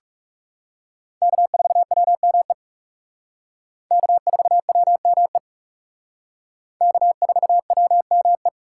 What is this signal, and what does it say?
Morse code, heard as a keyed tone, about 37 wpm, 700 Hz. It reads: K4WME K4WME K4WME
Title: Speed Racing - USA Call Signs 30wpm